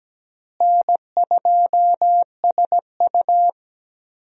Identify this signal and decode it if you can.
Morse code, heard as a keyed tone, about 17 wpm, 700 Hz. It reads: N2SU
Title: USA Call Signs - 4 Characters - Repeated 3x - 17wpm